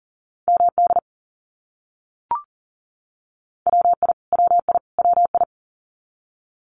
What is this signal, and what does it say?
Morse code, heard as a keyed tone, about 40 wpm, 700 Hz. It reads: MD WI WI WI
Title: US States Repeated 3x 40wpm